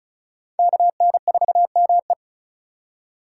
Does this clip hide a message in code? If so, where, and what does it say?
Yes: Morse code from 0.6 to 2.1 s, heard as a keyed tone, about 44 wpm, 700 Hz. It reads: KN4ME